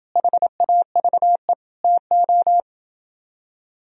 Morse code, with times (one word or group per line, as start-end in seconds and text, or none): 0.2-1.5 HAVE
1.8-2.6 TO